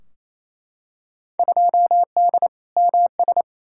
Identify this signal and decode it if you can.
Morse code, heard as a keyed tone, about 28 wpm, 700 Hz. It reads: 2D MS
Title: Field Day Exchanges Encore - 28wpm